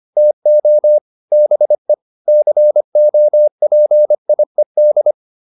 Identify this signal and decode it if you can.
Morse code, heard as a keyed tone, about 25 wpm, 600 Hz. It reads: TO BE COPIED